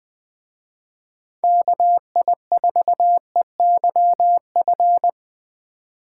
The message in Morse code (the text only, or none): KI4EYF